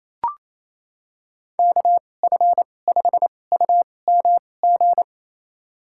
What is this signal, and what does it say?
Morse code, heard as a keyed tone, about 28 wpm, spaced as for 20 wpm, 700 Hz. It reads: KF5UMG
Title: USA Call Signs - 6 Characters - 2x Character Spacing - 28wpm